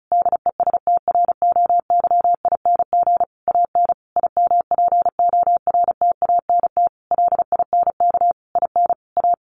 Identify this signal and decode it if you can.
Morse code, heard as a keyed tone, about 34 wpm, 700 Hz. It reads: DESTROYING AN IMPORTANT LINK IN A
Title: A Sound Of Thunder 35wpm